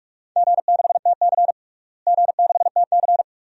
Morse code, heard as a keyed tone, about 41 wpm, 700 Hz. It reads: G6TC G6TC
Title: DXCC-100 Call Signs - 4 Characters - Repeated 2x - 45wpm